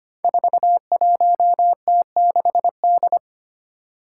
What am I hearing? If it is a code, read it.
Morse code, heard as a keyed tone, about 25 wpm, 700 Hz. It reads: 41T6D
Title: Five Challenging Letter-Number Combinations 25wpm